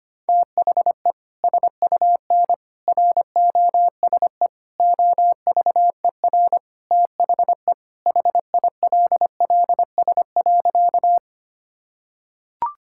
Morse code, heard as a keyed tone, about 25 wpm, 700 Hz. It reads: THE SUN ROSE OVER THE HILLS.
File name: practice / 25wpm / Sentences with 2-Word Phrases - 25wpm